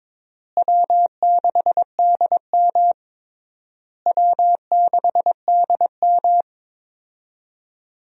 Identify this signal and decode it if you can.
Morse code, heard as a keyed tone, about 22 wpm, 700 Hz. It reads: W6DM W6DM
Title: USA Call Signs - 4 Characters - Repeated 3x - 22wpm